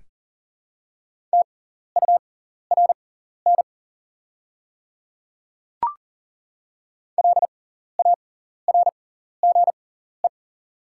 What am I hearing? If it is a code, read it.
Morse code, heard as a keyed tone, about 40 wpm, spaced as for 14 wpm, 690 Hz. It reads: TURN LARGE
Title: 6x Character Spacing - Top 500 Words - 40wpm